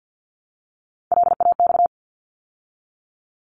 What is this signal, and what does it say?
Morse code, heard as a keyed tone, about 50 wpm, 700 Hz. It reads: LAX
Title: ARRL Sections 50wpm